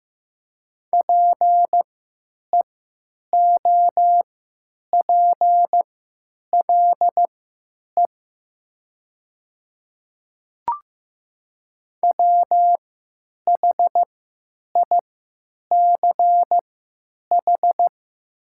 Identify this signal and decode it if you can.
Morse code, heard as a keyed tone, about 15 wpm, spaced as for 9 wpm, 700 Hz. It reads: PEOPLE WHICH
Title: 3x Character Spacing - Top 100 Words - 15wpm